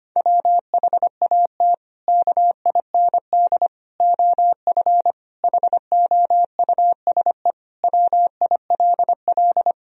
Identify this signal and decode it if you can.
Morse code, heard as a keyed tone, about 25 wpm, 700 Hz. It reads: WHAT KIND OF HOUSE WILL
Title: Sentences from Top 1000 Words - Review Part 1 - 25wpm